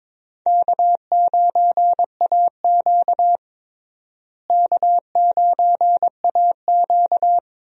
Morse code, heard as a keyed tone, about 21 wpm, 700 Hz. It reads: K9AQ K9AQ
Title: USA Call Signs - 4 Characters - Repeated 2x - 22wpm